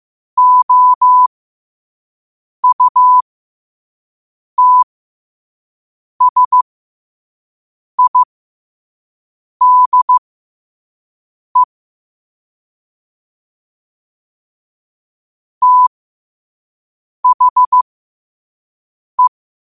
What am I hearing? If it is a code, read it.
Morse code, heard as a keyed tone, about 15 wpm, spaced as for 5 wpm, 1000 Hz. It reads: OUTSIDE THE
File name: qotdfiles / 5wpm